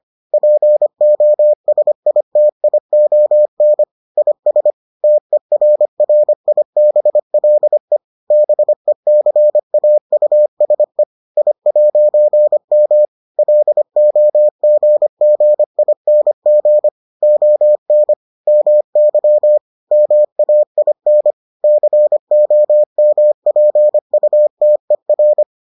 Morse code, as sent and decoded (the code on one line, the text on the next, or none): .--. --- ... .. - .. --- -. / .. ... / - . .-. .-. .. -... .-.. / -... -.-. .- ..- ... / .. .----. -- / .-.. --- --. --. .. -. --. / --- -. / -- -.-- / -- .- .. -. / -.-. --- -- .--. ..- - . .-.
POSITION IS TERRIBLE BECAUSE I'M LOGGING ON MY MAIN COMPUTER